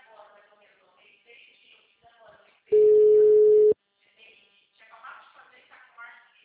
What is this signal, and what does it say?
Ringback tone (Europe, 425 Hz), with speech.